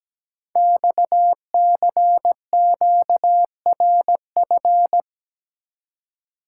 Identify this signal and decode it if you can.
Morse code, heard as a keyed tone, about 17 wpm, 700 Hz. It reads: XCQRF
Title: Code Groups - 5 Letters - 17wpm